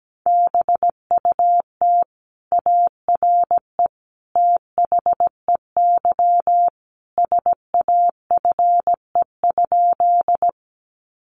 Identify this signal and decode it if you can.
Morse code, heard as a keyed tone, about 17 wpm, 700 Hz. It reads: BUT ARE THEY SAFE?
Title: Sentences from Top 900 Words 17wpm